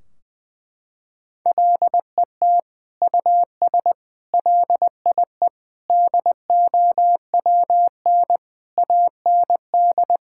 Morse code, heard as a keyed tone, about 21 wpm, 700 Hz. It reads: LET US LIE DOWN AND
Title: Sentences from Top 1600 Words 20wpm